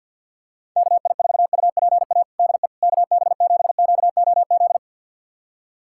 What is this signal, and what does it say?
Morse code, heard as a keyed tone, about 47 wpm, 700 Hz. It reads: KI4UPA DE KB7QOZ
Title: USA Callsign DE Callsign 50wpm